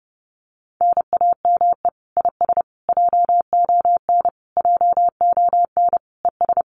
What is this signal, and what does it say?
Morse code, heard as a keyed tone, about 32 wpm, 700 Hz. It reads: NAME IS JON JONES
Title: Common QSO Phrases Encore 30wpm